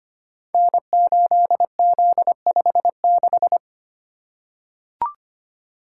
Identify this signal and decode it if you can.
Morse code, heard as a keyed tone, about 25 wpm, 700 Hz. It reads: N8Z56
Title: Code Groups - 5 Letter:Numbers - 25wpm